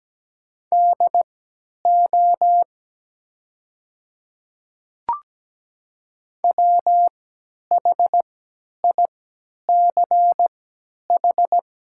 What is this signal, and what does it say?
Morse code, heard as a keyed tone, about 17 wpm, spaced as for 10 wpm, 700 Hz. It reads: DO WHICH
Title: 3x Character Spacing - Top 200 Words - 17wpm